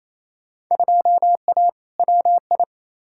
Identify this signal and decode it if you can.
Morse code, heard as a keyed tone, about 28 wpm, 700 Hz. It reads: 2A WI
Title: Field Day Exchanges Encore - 28wpm